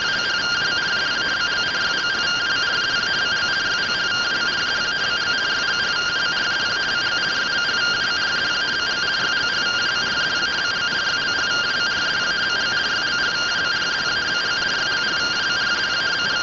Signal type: two-tone teleprinter signal

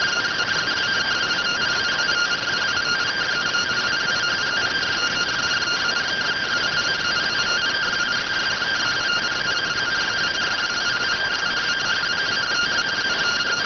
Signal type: two-tone teleprinter signal

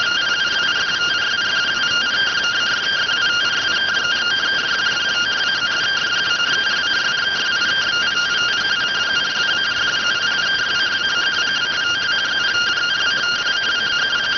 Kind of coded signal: two-tone teleprinter signal

